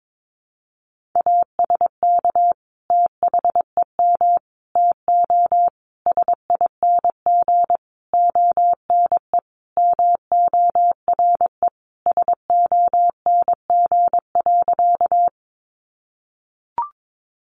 Morse code, heard as a keyed tone, about 22 wpm, 700 Hz. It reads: ASK THEM TO SING ONE MORE SONG.
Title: Sentences from Top 400 Words 22wpm